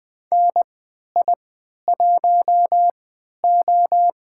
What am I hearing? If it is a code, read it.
Morse code, heard as a keyed tone, about 21 wpm, spaced as for 12 wpm, 700 Hz. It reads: NI1O
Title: USA Call Signs - 4 Characters - 3x Character Spacing - 20wpm